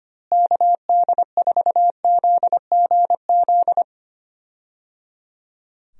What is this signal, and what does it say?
Morse code, heard as a keyed tone, about 25 wpm, 700 Hz. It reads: KD4ZGZ
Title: USA Call Signs - 6 Characters - Repeated 3x - 25wpm